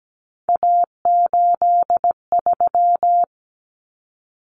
Morse code, heard as a keyed tone, about 17 wpm, 700 Hz. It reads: A83
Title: Common Cut Numbers - 3-Digit 17wpm